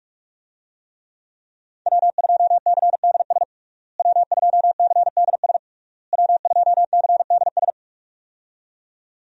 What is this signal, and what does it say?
Morse code, heard as a keyed tone, about 46 wpm, 700 Hz. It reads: W2CDS W2CDS W2CDS
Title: USA Call Signs - 5 Characters - Repeated 3x - 45wpm